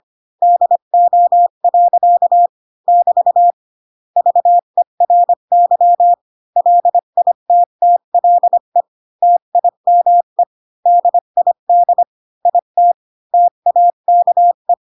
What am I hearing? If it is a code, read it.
Morse code, heard as a keyed tone, about 25 wpm, 700 Hz. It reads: DO. <BT> VERY LITTLE TIME DID IT TAKE
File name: Pinocchio / Pinocchio ch4 20 25 ik1hge